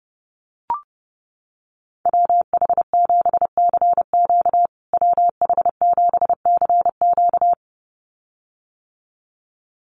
Morse code, heard as a keyed tone, about 28 wpm, 700 Hz. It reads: WH7CQ WH7CQ
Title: USA Call Signs Repeated 2x 30wpm